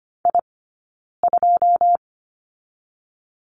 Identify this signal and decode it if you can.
Morse code, heard as a keyed tone, about 25 wpm, 700 Hz. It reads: I 2
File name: Lesson 17d - Letter u - Two Characters 25wpm